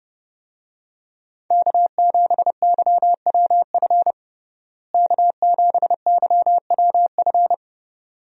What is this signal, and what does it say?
Morse code, heard as a keyed tone, about 30 wpm, 700 Hz. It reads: K7YWF K7YWF
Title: USA Call Signs - 5 Characters - Repeated 2x - 30wpm